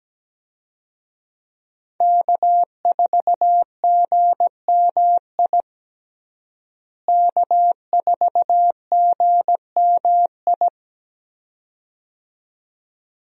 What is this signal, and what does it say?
Morse code, heard as a keyed tone, about 17 wpm, 700 Hz. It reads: K4GMI K4GMI
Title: USA Call Signs - 5 Characters - Repeated 2x - 17wpm